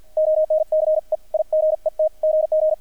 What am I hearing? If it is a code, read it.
Morse code, heard as a keyed tone, about 50 wpm, 620 Hz, with radio noise. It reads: ONCE I GET GG
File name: archived / CRSnet